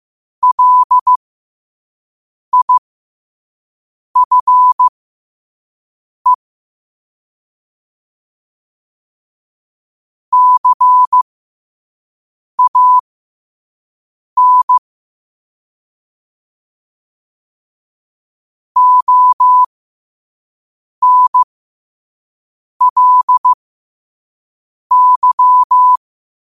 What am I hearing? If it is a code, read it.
Morse code, heard as a keyed tone, about 15 wpm, spaced as for 5 wpm, 1000 Hz. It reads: LIFE CAN ONLY
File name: qotdfiles / 5wpm